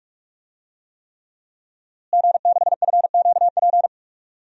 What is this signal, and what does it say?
Morse code, heard as a keyed tone, about 46 wpm, 700 Hz. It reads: G6FQP
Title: DXCC-100 Call Signs - 5 Characters - Repeated 3x - 45wpm